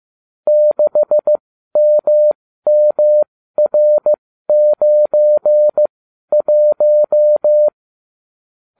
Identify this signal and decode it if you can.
Morse code, heard as a keyed tone, about 15 wpm, spaced as for 12 wpm, 600 Hz. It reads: 6MMR91